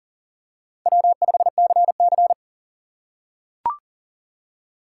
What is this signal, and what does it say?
Morse code, heard as a keyed tone, about 40 wpm, 700 Hz. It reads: W5CC
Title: USA Call Signs - 6x Character Spacing - 40wpm